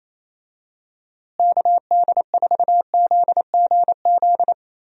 Morse code, heard as a keyed tone, about 28 wpm, 700 Hz. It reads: KD4ZGZ
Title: USA Call Signs - 6 Characters - Repeated 3x - 28wpm